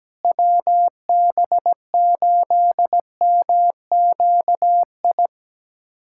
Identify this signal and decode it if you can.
Morse code, heard as a keyed tone, about 17 wpm, 700 Hz. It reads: WB8MQI
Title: USA Callsign DE Callsign - 6 Characters - 17wpm